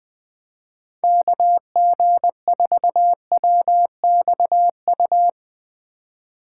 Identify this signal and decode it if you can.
Morse code, heard as a keyed tone, about 20 wpm, 700 Hz. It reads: KG4WXU
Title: 2023 POTA Hunter Call Signs - 20wpm